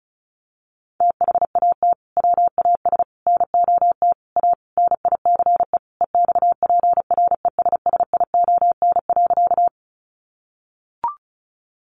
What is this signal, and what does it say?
Morse code, heard as a keyed tone, about 34 wpm, 700 Hz. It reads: THAT WAS NOT A NICE EXPRESSION.
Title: Sentences from Top 1200 Words 35wpm